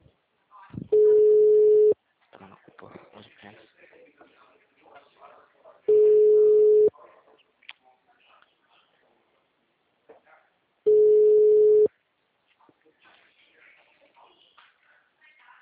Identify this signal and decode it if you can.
Ringback tone (Europe, 425 Hz), with speech.